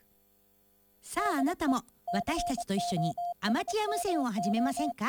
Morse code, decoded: DE COSMO